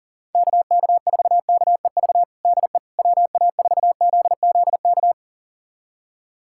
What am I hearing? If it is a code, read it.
Morse code, heard as a keyed tone, about 40 wpm, 700 Hz. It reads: KK4KEV DE WA4ZZK
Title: USA Callsign DE Callsign - 6 Characters - 40wpm